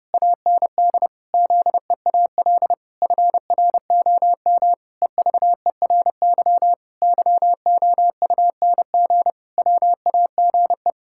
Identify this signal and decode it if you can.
Morse code, heard as a keyed tone, about 30 wpm, 700 Hz. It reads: AND ZEAL FROM EVERY YOUNG WAGE
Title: Pangram Challenge v2 - 30wpm